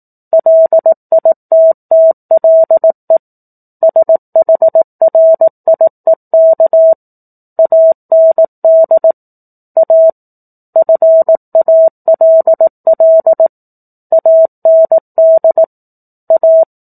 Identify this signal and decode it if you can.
Morse code, heard as a keyed tone, about 18 wpm, 650 Hz. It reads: LITTLE SHRIEK AND A FALL AND A
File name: 04-Chapter-04 18 WPM